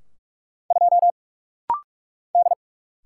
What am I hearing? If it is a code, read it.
Morse code, heard as a keyed tone, about 44 wpm, 700 Hz. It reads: 2 D
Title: Single Letter-Number - Rapid-Fire 45wpm